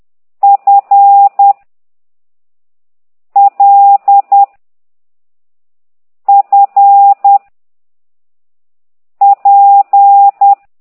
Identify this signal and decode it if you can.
Morse code, heard as a keyed tone, about 10 wpm, 800 Hz. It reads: F L F P
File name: ct2idw / exerc9